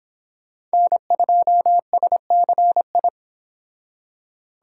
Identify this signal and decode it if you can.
Morse code, heard as a keyed tone, about 25 wpm, 700 Hz. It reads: N2SCI